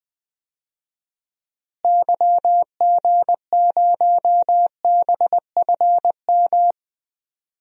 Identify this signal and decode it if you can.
Morse code, heard as a keyed tone, about 21 wpm, 700 Hz. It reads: YG0BFM
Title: DXCC-100 Call Signs - 6 Characters - Repeated 2x - 20wpm